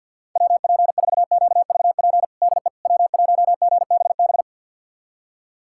Morse code, heard as a keyed tone, about 50 wpm, 700 Hz. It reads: WP4QVP DE W1GBB
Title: USA Callsign DE Callsign 50wpm